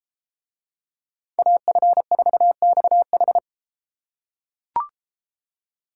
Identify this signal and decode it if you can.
Morse code, heard as a keyed tone, about 31 wpm, 700 Hz. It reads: AF4XH